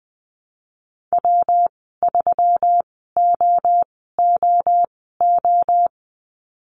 Morse code, heard as a keyed tone, about 20 wpm, spaced as for 15 wpm, 700 Hz. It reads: W3OOO
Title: 122 Lesson 20g - Number 3 - Call Signs 20wpm